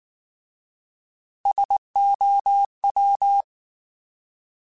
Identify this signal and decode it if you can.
Morse code, heard as a keyed tone, about 19 wpm, 800 Hz. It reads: SOW
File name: Alphabetical / Alphabetical mp3s / Ss @18